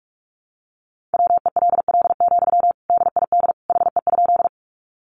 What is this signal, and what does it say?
Morse code, heard as a keyed tone, about 44 wpm, 700 Hz. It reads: WELL, DID HE?
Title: Sentences from Top 200 Words 45wpm